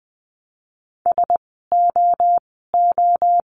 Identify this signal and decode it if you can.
Morse code, heard as a keyed tone, about 20 wpm, 700 Hz. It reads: S O O
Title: 036 Lesson 7f - Letter s - Words 20wpm